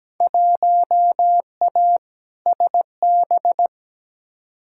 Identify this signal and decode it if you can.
Morse code, heard as a keyed tone, about 17 wpm, 700 Hz. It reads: 1A SB
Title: Field Day Exchanges Encore - 17wpm